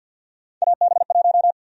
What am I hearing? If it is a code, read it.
Morse code, heard as a keyed tone, about 52 wpm, 700 Hz. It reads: AB1
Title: USA Call Sign Prefixes - Repeated 2x - 50wpm